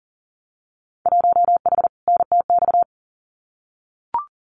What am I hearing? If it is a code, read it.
Morse code, heard as a keyed tone, about 39 wpm, 700 Hz. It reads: 1H NTX
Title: Winter Field Day Exchanges 40wpm